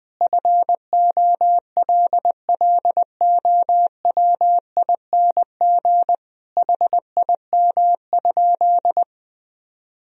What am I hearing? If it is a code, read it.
Morse code, heard as a keyed tone, about 20 wpm, 700 Hz. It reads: FOLLOWING HIM?
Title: Sentences from Top 1000 Words - Review Part 4 - 20wpm